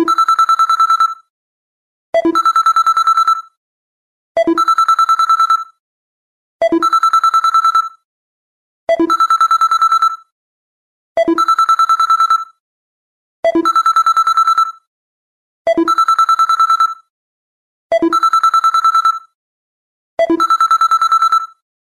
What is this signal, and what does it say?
There is no code.